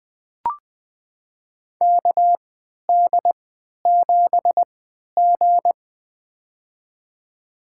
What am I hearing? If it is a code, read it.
Morse code, heard as a keyed tone, about 21 wpm, spaced as for 12 wpm, 700 Hz. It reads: KD7G
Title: USA Call Signs - 4 Characters - 3x Character Spacing - 20wpm